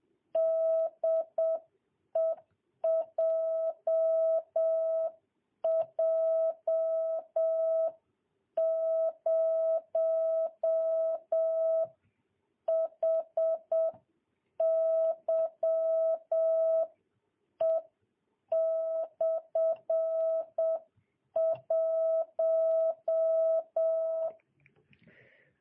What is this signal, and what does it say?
Morse code, heard as a keyed tone, about 7 wpm, 660 Hz. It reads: DE JJ0HYE/1
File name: jh8cbh / 2022cw